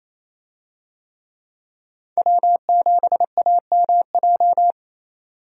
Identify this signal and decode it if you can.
Morse code, heard as a keyed tone, about 28 wpm, 700 Hz. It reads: W7AMJ
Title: USA Call Signs - 5 Characters - 28wpm